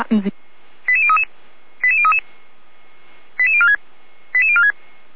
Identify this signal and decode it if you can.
Five-tone selective call (ZVEI), heard as quick runs of five tones, tones of about 80 ms, with speech. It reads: 88033 88033 88046 88046